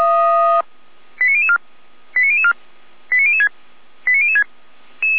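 Five-tone selective call (ZVEI), heard as quick runs of five tones, tones of about 70 ms. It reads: 89004 89004 89006 89006
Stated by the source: ZVEI five-tone selective call 89010 received by SDR, Feuerwehr Orlamünde fire brigade alerting.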